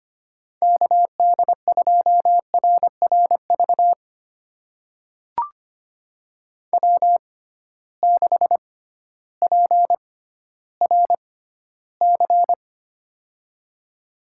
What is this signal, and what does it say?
Morse code, heard as a keyed tone, about 25 wpm, spaced as for 13 wpm, 700 Hz. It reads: KD2RRV W6PRC
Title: USA Call Signs - 6x Character Spacing - 25wpm